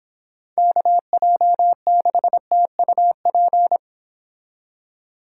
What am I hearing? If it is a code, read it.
Morse code, heard as a keyed tone, about 26 wpm, 700 Hz. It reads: KJ6TUP